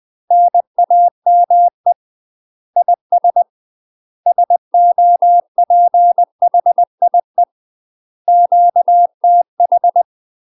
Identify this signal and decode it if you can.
Morse code, heard as a keyed tone, about 20 wpm, 700 Hz. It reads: NAME IS SOPHIE QTH
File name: UK MASTER QSOs TRAINING / 20WPM / UK MASTER QSOs 0017